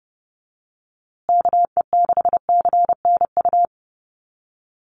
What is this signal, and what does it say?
Morse code, heard as a keyed tone, about 30 wpm, 700 Hz. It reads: KE6CNU
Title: USA Call Signs Repeated 2x 30wpm